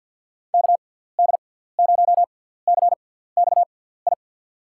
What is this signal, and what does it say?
Morse code, heard as a keyed tone, about 50 wpm, spaced as for 17 wpm, 700 Hz. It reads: KD0CXI
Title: USA Call Signs - 6 Characters - 6x Character Spacing - 50wpm